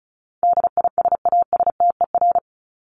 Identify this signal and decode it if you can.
Morse code, heard as a keyed tone, about 34 wpm, 700 Hz. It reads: DISASTER
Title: Top 1501-1750 Words 35wpm